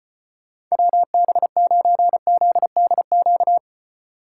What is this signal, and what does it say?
Morse code, heard as a keyed tone, about 31 wpm, 700 Hz. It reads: WB9ZDQ
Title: Speed Racing - USA Call Signs 25wpm